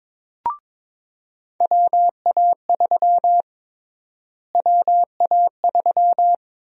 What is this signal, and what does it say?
Morse code, heard as a keyed tone, about 21 wpm, 700 Hz. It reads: WA3 WA3
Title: USA Call Sign Prefixes - Repeated 2x - 22wpm